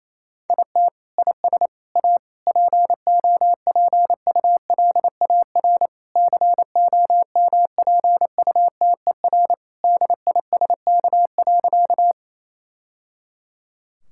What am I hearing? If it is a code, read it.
Morse code, heard as a keyed tone, about 27 wpm, 700 Hz. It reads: IT IS A POPULAR COMPUTER DISK.
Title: Sentences from Top 1000 Words - Review Part 6 - 28wpm